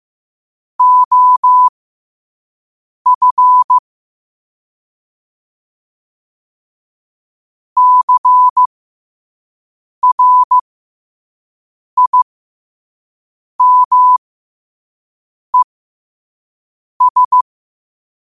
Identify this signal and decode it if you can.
Morse code, heard as a keyed tone, about 15 wpm, spaced as for 5 wpm, 1000 Hz. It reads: OF CRIMES
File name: qotdfiles / 5wpm